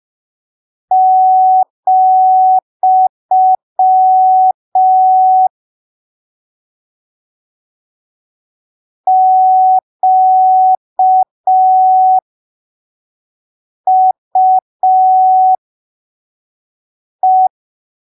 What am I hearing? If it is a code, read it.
Morse code, heard as a keyed tone, about 5 wpm, spaced as for 3 wpm, 750 Hz. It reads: , QUE
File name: arquivos / texto / texto 02 5ppm